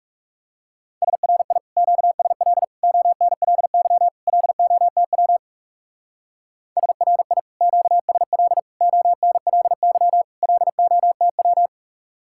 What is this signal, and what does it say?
Morse code, heard as a keyed tone, about 36 wpm, 700 Hz. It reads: SRI QSL ONLY LOTW SRI QSL ONLY LOTW
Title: Speed-Racing Common QSO Phrases Encore - 30wpm